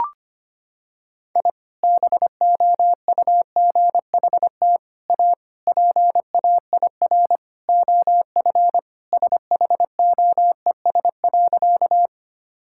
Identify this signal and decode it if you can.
Morse code, heard as a keyed tone, about 25 wpm, 700 Hz. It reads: I BOUGHT A PAIR OF SHOES.
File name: practice / 25wpm / Sentences with 3-Word Phrases - 25wpm